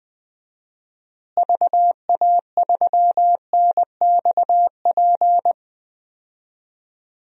Morse code, heard as a keyed tone, about 21 wpm, 700 Hz. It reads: VA3NXP
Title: DXCC-100 Call Signs - 6 Characters - Repeated 2x - 20wpm